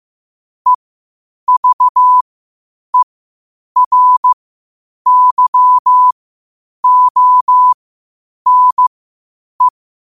Morse code, heard as a keyed tone, about 15 wpm, spaced as for 8 wpm, 1000 Hz. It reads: EVERYONE
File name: qotdfiles / 8wpm